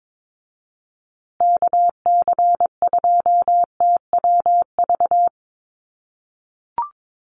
Morse code, heard as a keyed tone, about 22 wpm, 700 Hz. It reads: KC2TWV
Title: USA Call Signs Repeated 3x 22wpm